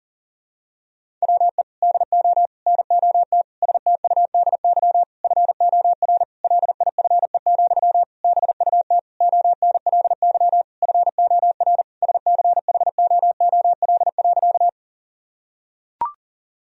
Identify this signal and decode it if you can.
Morse code, heard as a keyed tone, about 40 wpm, 700 Hz. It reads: WE DO NOT STUDY FOR LIFE, BUT ONLY FOR SCHOOL.
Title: Sentences from Top 1000 Words - Review Part 1 - 40wpm